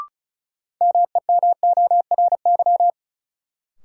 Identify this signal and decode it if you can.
Morse code, heard as a keyed tone, about 36 wpm, 700 Hz. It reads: MEMORY
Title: Sentences from Top 1300 Words - New Word Drill Alt - 35wpm